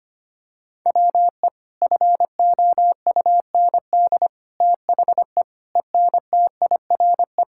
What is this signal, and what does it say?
Morse code, heard as a keyed tone, about 25 wpm, 700 Hz. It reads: WE FOUND THE ENTIRE
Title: Sentences from Top 1000 Words - Review Part 5 - 25wpm